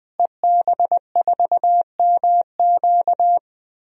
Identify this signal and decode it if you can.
Morse code, heard as a keyed tone, about 21 wpm, 700 Hz. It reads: EB4MQ
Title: DXCC-100 Call Signs - 5 Characters - Repeated 3x - 20wpm